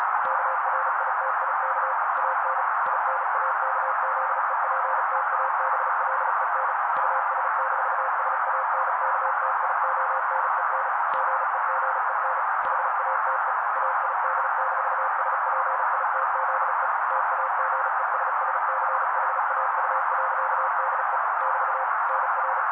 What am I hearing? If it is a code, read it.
Morse code, heard as a keyed tone, about 35 wpm, 530 Hz. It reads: KRITIK AN INAKZEPTABLEN US-SANKTIONEN GEGEN HATE AID-SPITZE TAGESSCHAU.DE <BT> NR
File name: daily / 35wpm noise